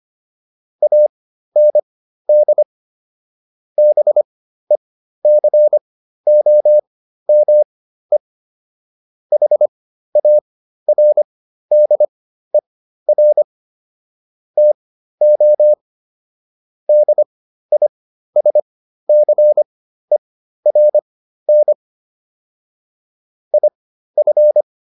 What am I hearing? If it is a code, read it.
Morse code, heard as a keyed tone, about 25 wpm, spaced as for 13 wpm, 600 Hz. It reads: AND BECOME HARDER TO DISCERN IF